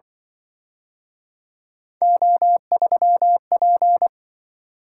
Morse code, heard as a keyed tone, about 25 wpm, 700 Hz. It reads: O3P
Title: Speed-Racing - Three Letter-Number Combinations - 20wpm